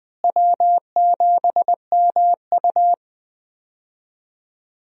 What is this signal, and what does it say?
Morse code, heard as a keyed tone, about 20 wpm, 700 Hz. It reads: W7MU
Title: USA Call Signs - 4 Characters - Repeated 2x - 20wpm